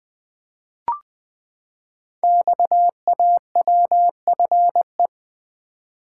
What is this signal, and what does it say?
Morse code, heard as a keyed tone, about 20 wpm, 700 Hz. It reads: XAWFE
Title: Code Groups - 5 Letters - 20wpm